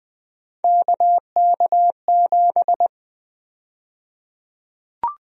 Morse code, heard as a keyed tone, about 20 wpm, 700 Hz. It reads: KK7
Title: USA Call Sign Prefixes - Repeated 2x - 20wpm